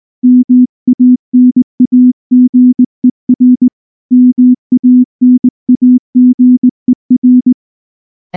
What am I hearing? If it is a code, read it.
Morse code, heard as a keyed tone, about 20 wpm, 250 Hz. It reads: MANAGER MANAGER
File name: morse / morse-training-20WPM